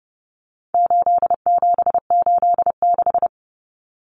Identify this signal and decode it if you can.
Morse code, heard as a keyed tone, about 30 wpm, 700 Hz. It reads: 8786
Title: Numbers - 4-Digit 30wpm